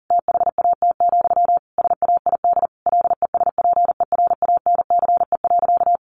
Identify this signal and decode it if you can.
Morse code, heard as a keyed tone, about 40 wpm, 700 Hz. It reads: THAT, SAID LESPERANCE.
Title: A Sound Of Thunder 40wpm